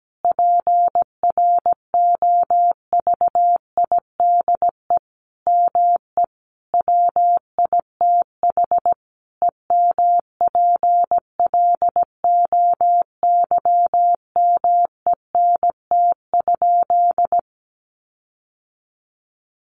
Morse code, heard as a keyed tone, about 17 wpm, 700 Hz. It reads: PROVIDE ME WITH EMPLOYMENT?
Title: Sentences from Top 1500 Words 17wpm